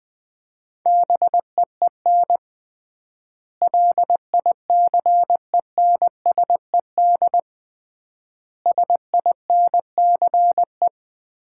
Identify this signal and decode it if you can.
Morse code, heard as a keyed tone, about 20 wpm, 700 Hz. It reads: BEEN LICENSED SINCE
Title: Common QSO Phrases - 3x Word-Spacing - 20wpm